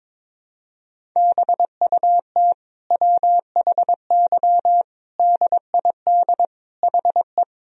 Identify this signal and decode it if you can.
Morse code, heard as a keyed tone, about 22 wpm, 700 Hz. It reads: BUT WHY DID HE
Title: Sentences from Top 1000 Words - Review Part 2 - 22wpm